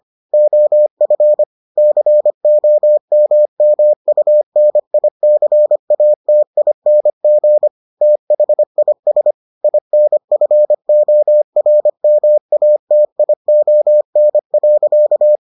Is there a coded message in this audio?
Yes: Morse code from 0.3 to 15.4 s, heard as a keyed tone, about 25 wpm, 600 Hz. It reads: OF COMMUNICATING THIS INFORMATION.